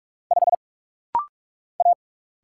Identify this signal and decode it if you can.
Morse code, heard as a keyed tone, about 44 wpm, 700 Hz. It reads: 5 A
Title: Single Letter-Number Alt - Rapid-Fire 45wpm